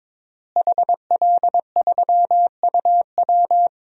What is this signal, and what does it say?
Morse code, heard as a keyed tone, about 22 wpm, 700 Hz. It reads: HL3UW
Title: DXCC-100 Call Signs - 5 Characters - Repeated 2x - 22wpm